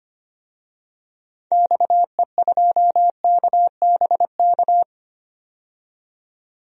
Morse code, heard as a keyed tone, about 25 wpm, 700 Hz. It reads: XE2KBK
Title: DXCC-100 Call Signs - 6 Characters - 25wpm